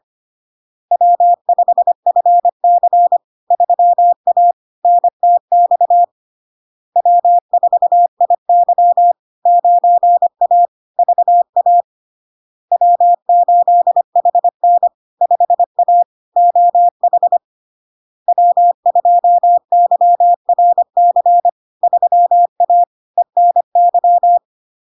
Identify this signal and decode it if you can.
Morse code, heard as a keyed tone, about 25 wpm, 700 Hz. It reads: W5FC 3A NTX W4IY 9A VA W8HN 5A OH W2YRC 3A ENY
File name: LARK-FD-2022-Log-in-CW-25wpm-700-Hz